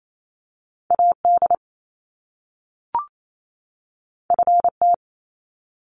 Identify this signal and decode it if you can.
Morse code, heard as a keyed tone, about 28 wpm, 700 Hz. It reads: AD FT
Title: Top Two Letter Combinations in Words 28wpm